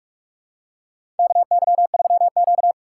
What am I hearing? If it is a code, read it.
Morse code, heard as a keyed tone, about 44 wpm, 700 Hz. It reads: KY3Q
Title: USA Call Signs - 4 Characters - 6x Character Spacing - 45wpm